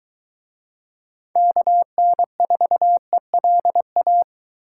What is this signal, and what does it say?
Morse code, heard as a keyed tone, about 23 wpm, 700 Hz. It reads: KN4ELA